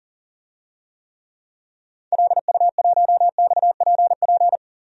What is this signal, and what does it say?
Morse code, heard as a keyed tone, about 37 wpm, 700 Hz. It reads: LU1XPP
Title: DXCC-100 Call Signs - 6 Characters - 40wpm